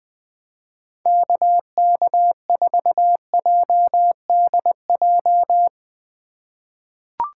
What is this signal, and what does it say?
Morse code, heard as a keyed tone, about 20 wpm, 700 Hz. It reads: KK4JDJ
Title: USA Call Signs - 6 Characters - 2x Character Spacing - 20wpm